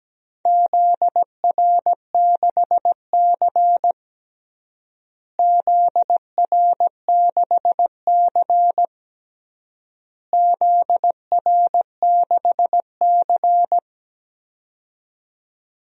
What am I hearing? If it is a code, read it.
Morse code, heard as a keyed tone, about 17 wpm, 700 Hz. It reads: ZR6C ZR6C ZR6C
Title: DXCC-100 Call Signs - 4 Characters - Repeated 3x - 17wpm